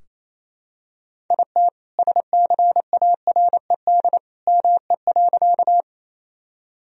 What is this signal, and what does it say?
Morse code, heard as a keyed tone, about 28 wpm, 700 Hz. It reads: IT SCARED ME.
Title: Sentences from Top 1000 Words - Review Part 4 - 28wpm